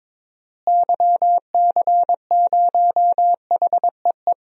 Morse code, heard as a keyed tone, about 22 wpm, 700 Hz. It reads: YC0HEE
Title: DXCC-100 Call Signs - 6 Characters - Repeated 2x - 22wpm